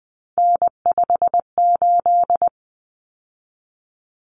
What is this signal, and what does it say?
Morse code, heard as a keyed tone, about 20 wpm, 700 Hz. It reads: N58
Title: Common Cut Numbers - 3-Digit 20wpm